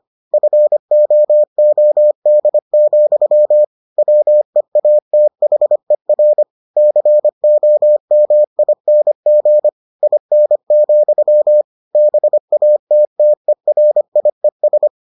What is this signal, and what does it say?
Morse code, heard as a keyed tone, about 25 wpm, 600 Hz. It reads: FOOD, WEATHER COMING IN, BATTERIES